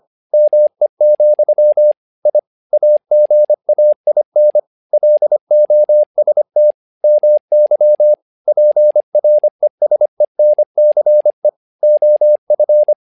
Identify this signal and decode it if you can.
Morse code, heard as a keyed tone, about 25 wpm, 600 Hz. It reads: ME, I AGAIN LOST MY PRESENCE OF
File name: learning center / tower trouble / tower trouble 25wpm